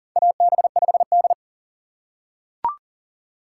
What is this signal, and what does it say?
Morse code, heard as a keyed tone, about 40 wpm, 700 Hz. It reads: AB5D
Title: USA Call Signs - 3x Character Spacing - 40wpm